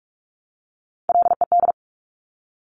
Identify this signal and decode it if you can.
Morse code, heard as a keyed tone, about 46 wpm, 700 Hz. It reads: LED
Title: Top Three Letter Combinations in Words 45wpm